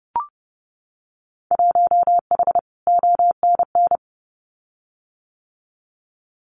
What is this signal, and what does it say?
Morse code, heard as a keyed tone, about 30 wpm, 700 Hz. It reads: 1H ONN
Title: Winter Field Day Exchanges 30wpm